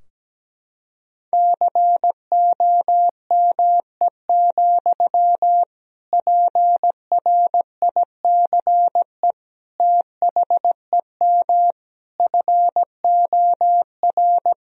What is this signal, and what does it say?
Morse code, heard as a keyed tone, about 17 wpm, 700 Hz. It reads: COME, PRICE THEM FOR
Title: Sentences from Top 1000 Words - Review Part 4 - 17wpm